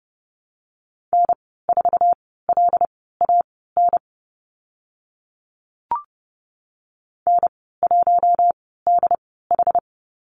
Morse code, heard as a keyed tone, about 30 wpm, spaced as for 17 wpm, 700 Hz. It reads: N4LAN N1DH
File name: course / 30:10wpm / 088 Lesson 15g - Number 5 - Call Signs 30wpm